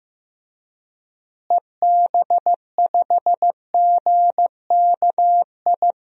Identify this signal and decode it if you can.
Morse code, heard as a keyed tone, about 15 wpm, 700 Hz. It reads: EB5GKI
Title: DXCC-100 Call Signs - 6 Characters - Repeated 2x - 15wpm